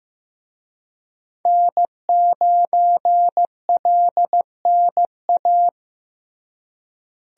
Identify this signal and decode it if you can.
Morse code, heard as a keyed tone, about 15 wpm, 700 Hz. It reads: N9LNA
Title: USA Call Signs - 5 Characters - Repeated 2x - 15wpm